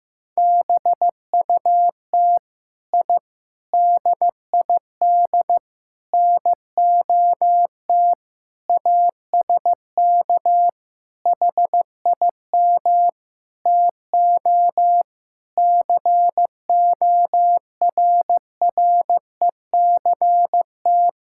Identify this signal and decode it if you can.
Morse code, heard as a keyed tone, about 15 wpm, 700 Hz. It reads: BUT I DID NOT ASK HIM TO CORRECT
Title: Sentences from Top 1000 Words - Review Part 5 - 15wpm